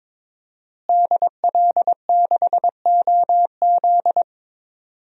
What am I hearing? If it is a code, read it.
Morse code, heard as a keyed tone, about 21 wpm, 700 Hz. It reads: DL6OZ
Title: DXCC-100 Call Signs - 5 Characters - Repeated 2x - 22wpm